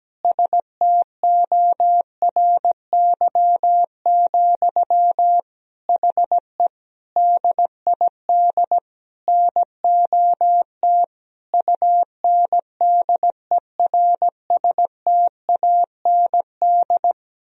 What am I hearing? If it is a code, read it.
Morse code, heard as a keyed tone, about 17 wpm, 700 Hz. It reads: STORY, HE DID NOT UNDERSTAND
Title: Sentences from Top 1000 Words - Review Part 3 - 17wpm